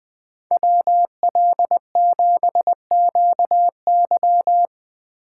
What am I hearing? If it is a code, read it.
Morse code, heard as a keyed tone, about 20 wpm, 700 Hz. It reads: WL7QY
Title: Speed Racing - USA 2x2 Call Signs 17wpm